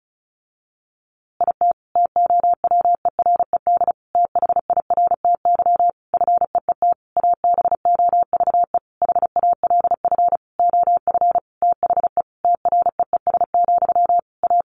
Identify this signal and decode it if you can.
Morse code, heard as a keyed tone, about 34 wpm, 700 Hz. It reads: IT TOWERED THIRTY FEET ABOVE HALF OF THE TREES, A
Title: A Sound Of Thunder 35wpm